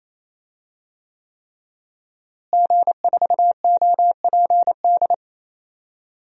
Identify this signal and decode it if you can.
Morse code, heard as a keyed tone, about 28 wpm, 700 Hz. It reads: G4OPD
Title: DXCC-100 Call Signs - 5 Characters - Repeated 3x - 28wpm